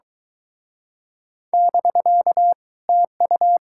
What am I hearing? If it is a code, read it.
Morse code, heard as a keyed tone, about 26 wpm, 700 Hz. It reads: <BK> TU